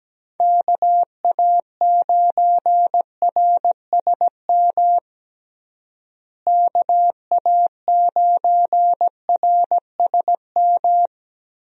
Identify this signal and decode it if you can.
Morse code, heard as a keyed tone, about 17 wpm, 700 Hz. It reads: KA9RSM KA9RSM
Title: USA Call Signs - 6 Characters - Repeated 3x - 17wpm